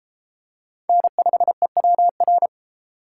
Morse code, heard as a keyed tone, about 32 wpm, 700 Hz. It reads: N5EWR